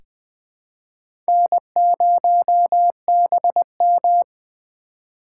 Morse code, heard as a keyed tone, about 20 wpm, 700 Hz. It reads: N0BM
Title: USA Call Signs - 4 Characters - 2x Character Spacing - 20wpm